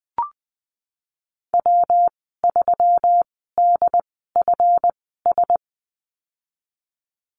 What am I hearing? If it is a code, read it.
Morse code, heard as a keyed tone, about 20 wpm, spaced as for 15 wpm, 700 Hz. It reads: W3DFS
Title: 161 Lesson 26g - Letter g - Call Signs 20wpm